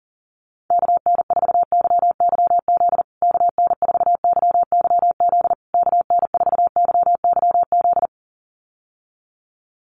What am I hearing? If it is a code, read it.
Morse code, heard as a keyed tone, about 40 wpm, 700 Hz. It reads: KN4YYZ KN4YYZ KN4YYZ
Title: USA Call Signs Repeated 3x 40wpm